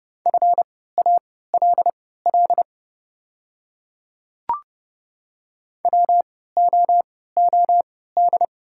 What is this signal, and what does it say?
Morse code, heard as a keyed tone, about 31 wpm, spaced as for 17 wpm, 700 Hz. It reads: FALL WOOD
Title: 3x Character Spacing - Top 500 Words - 30wpm